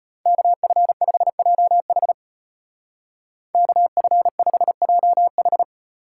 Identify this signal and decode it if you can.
Morse code, heard as a keyed tone, about 31 wpm, 700 Hz. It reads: KF5JH KF5JH
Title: Speed Racing - USA 2x2 Call Signs 25wpm